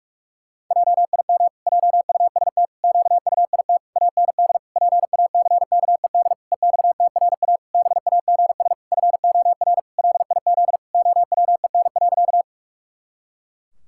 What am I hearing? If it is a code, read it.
Morse code, heard as a keyed tone, about 41 wpm, 700 Hz. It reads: JIM JUST QUIT AND PACKED EXTRA BAGS FOR LIZ OWEN.
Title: Pangram Challenge v2 - 45wpm